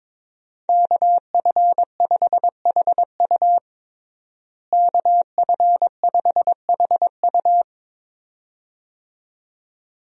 Morse code, heard as a keyed tone, about 22 wpm, 700 Hz. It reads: KF5HU KF5HU
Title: USA Call Signs - 5 Characters - Repeated 2x - 22wpm